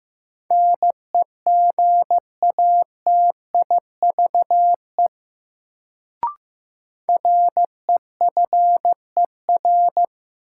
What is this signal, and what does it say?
Morse code, heard as a keyed tone, about 15 wpm, 700 Hz. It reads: NEGATIVE REFER
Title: Sentences from Top 1000 Words - New Word Drill - 15wpm